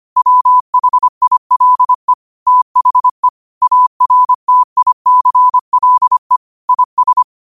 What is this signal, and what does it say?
Morse code, heard as a keyed tone, about 25 wpm, 1000 Hz. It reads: WHILE THE ARTICLE IS
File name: qotdfiles / 25wpm